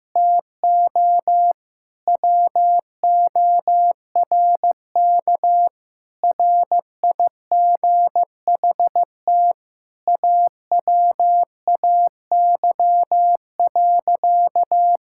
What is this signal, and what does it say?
Morse code, heard as a keyed tone, about 15 wpm, 700 Hz. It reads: TO WORK RIGHT AWAY.
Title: Sentences from Top 1600 Words 15wpm